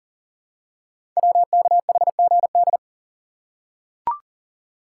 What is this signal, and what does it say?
Morse code, heard as a keyed tone, about 38 wpm, 700 Hz. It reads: WKHGD
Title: Code Groups - 5 Letters - 40wpm